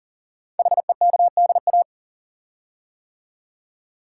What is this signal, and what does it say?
Morse code, heard as a keyed tone, about 41 wpm, 700 Hz. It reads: HEKDA
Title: Code Groups - 5 Letters - 40wpm